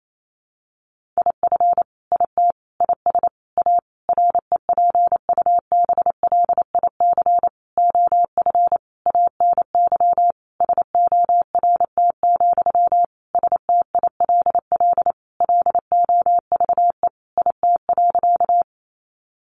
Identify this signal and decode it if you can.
Morse code, heard as a keyed tone, about 28 wpm, 700 Hz. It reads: IF IT IS A REPUBLIC OF ANY SORT, STILL LOVE IT.
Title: Sentences from Top 1500 Words 28wpm